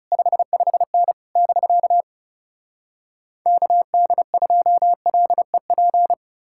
Morse code, heard as a keyed tone, about 40 wpm, 700 Hz. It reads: 55N <BK> KD2LEP